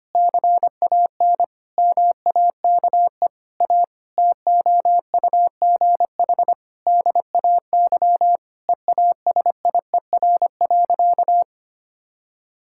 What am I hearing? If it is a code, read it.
Morse code, heard as a keyed tone, about 25 wpm, 700 Hz. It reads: CAN MAKE A TOUGH DAY EASIER.
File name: practice / 25wpm / Difficult Sentences with Hints - 25wpm